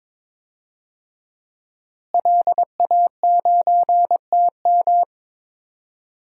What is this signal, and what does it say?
Morse code, heard as a keyed tone, about 21 wpm, 700 Hz. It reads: LA9TM
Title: DXCC-100 Call Signs - 5 Characters - Repeated 2x - 22wpm